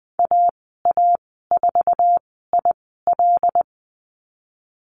Morse code, heard as a keyed tone, about 20 wpm, 700 Hz. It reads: A A 4 I L
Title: 082 Lesson 14g - Number 2 - Call Signs 20wpm